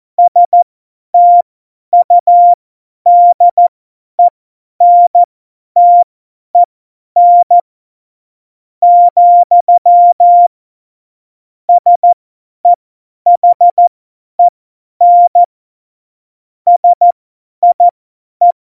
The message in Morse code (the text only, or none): STUDENTEN, SEHEN SIE